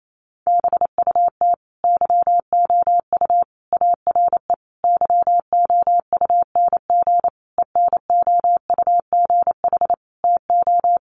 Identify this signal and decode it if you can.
Morse code, heard as a keyed tone, about 28 wpm, 700 Hz. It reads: BUT YOU ARE YOUNG ENOUGH TO
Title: Sentences from Top 1200 Words 28wpm